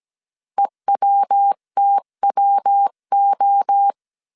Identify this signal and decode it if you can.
Morse code, heard as a keyed tone, about 17 wpm, 800 Hz. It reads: EWTWO
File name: CW / DET / 1 3 1500C S A 16wpm 2141